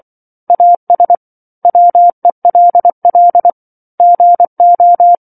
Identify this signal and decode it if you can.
Morse code, heard as a keyed tone, about 24 wpm, 700 Hz. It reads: AS WELL GO